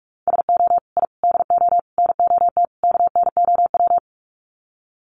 Morse code, heard as a keyed tone, about 44 wpm, 700 Hz. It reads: SO I DO NOT KNOW
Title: Sets of 5 Words Encore 45wpm